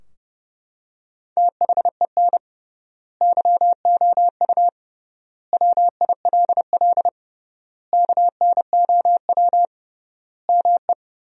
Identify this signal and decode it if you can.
Morse code, heard as a keyed tone, about 31 wpm, 700 Hz. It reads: THEN YOU WILL KNOW ME
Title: 3x Word-Spacing - Sentences from Top 400 Words - 30wpm